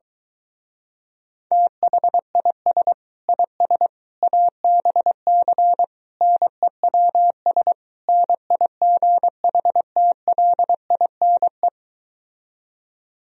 Morse code, heard as a keyed tone, about 22 wpm, 700 Hz. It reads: THIS IS ABC NEWS NIGHTLINE